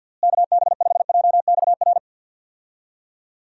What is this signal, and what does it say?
Morse code, heard as a keyed tone, about 49 wpm, 700 Hz. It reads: KB5JXR
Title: USA Call Signs - 6 Characters - Repeated 3x - 50wpm